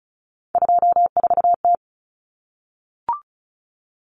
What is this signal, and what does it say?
Morse code, heard as a keyed tone, about 34 wpm, 700 Hz. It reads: 24T
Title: Common Cut Numbers - 3-Digit 35wpm